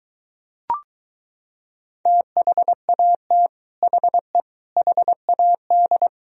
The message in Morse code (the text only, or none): THAT HE HAD